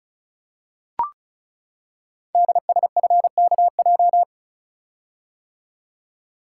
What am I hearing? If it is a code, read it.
Morse code, heard as a keyed tone, about 36 wpm, 690 Hz. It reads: DSFKJ
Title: Code Groups - 5 Letters - 35wpm